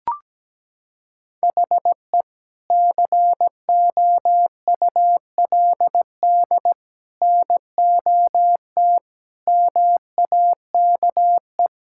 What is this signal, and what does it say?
Morse code, heard as a keyed tone, about 17 wpm, 700 Hz. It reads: HE COULD NOT MAKE
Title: Sentences from Top 1600 Words 17wpm